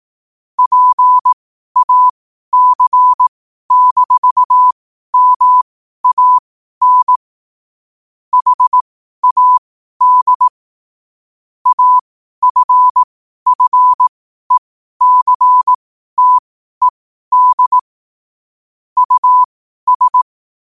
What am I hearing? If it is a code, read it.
Morse code, heard as a keyed tone, about 18 wpm, spaced as for 12 wpm, 1000 Hz. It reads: PAC-MAN HAD AFFECTED US